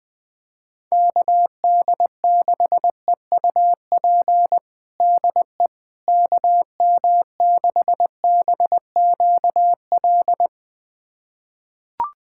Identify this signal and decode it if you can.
Morse code, heard as a keyed tone, about 20 wpm, 700 Hz. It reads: KD6EUP DE KM6BQL
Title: USA Callsign DE Callsign - 6 Characters - 20wpm